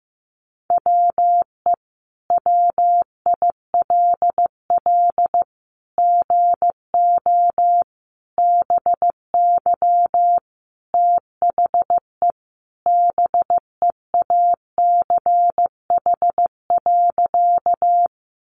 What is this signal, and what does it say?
Morse code, heard as a keyed tone, about 15 wpm, 700 Hz. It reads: WE WILL GO BY THE BEACH.
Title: Sentences from Top 1500 Words 15wpm